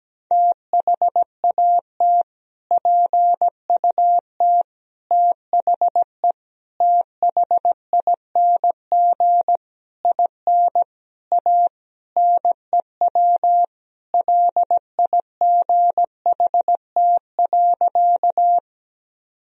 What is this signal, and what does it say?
Morse code, heard as a keyed tone, about 17 wpm, 700 Hz. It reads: THAT PUT THE THING IN A NEW LIGHT.
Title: Sentences from Top 1000 Words - Review Part 1 - 17wpm